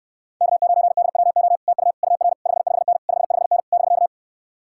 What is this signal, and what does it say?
Morse code, heard as a keyed tone, about 57 wpm, 700 Hz. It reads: K9DWO TU UR 55N 55N <BK>